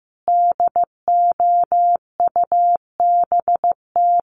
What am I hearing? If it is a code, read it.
Morse code, heard as a keyed tone, about 15 wpm, 700 Hz. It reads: DOUBT
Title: Sentences from Top 1500 Words 15wpm